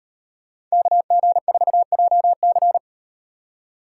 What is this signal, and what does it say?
Morse code, heard as a keyed tone, about 38 wpm, 700 Hz. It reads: KG4JC